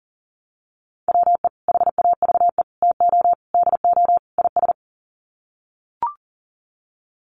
Morse code, heard as a keyed tone, about 40 wpm, 700 Hz. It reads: WE HAVE TO DO IS
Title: Sets of 5 Words Encore 40wpm